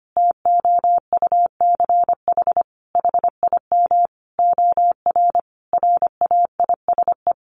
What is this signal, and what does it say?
Morse code, heard as a keyed tone, about 25 wpm, 700 Hz. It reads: TOUCH HIM OR RAISE
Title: Sentences from Top 1400 Words 25wpm